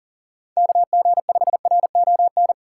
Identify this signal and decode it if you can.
Morse code, heard as a keyed tone, about 44 wpm, 700 Hz. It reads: KG5RON
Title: USA Call Signs - 6 Characters - 6x Character Spacing - 40wpm